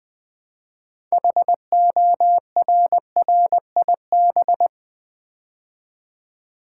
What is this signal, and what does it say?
Morse code, heard as a keyed tone, about 20 wpm, 700 Hz. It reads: HORRIB
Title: Build A Word - 3 Letters At A Time - 8-Letter Words - 20wpm